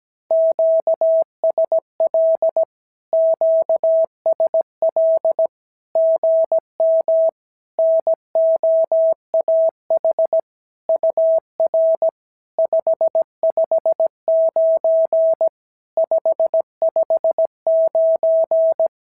Morse code, heard as a keyed tone, about 16 wpm, 650 Hz. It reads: QSL QSL GM NOAH UR 559 559
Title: SOTA QSO Practice 17wpm